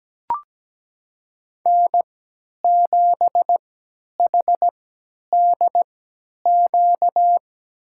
Morse code, heard as a keyed tone, about 17 wpm, spaced as for 10 wpm, 700 Hz. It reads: N7HDQ
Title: USA Call Signs - 3x Character Spacing - 17wpm